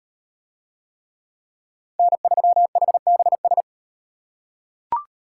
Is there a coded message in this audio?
Yes: Morse code from 2.0 to 3.6 s, heard as a keyed tone, about 38 wpm, 700 Hz. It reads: N3HBS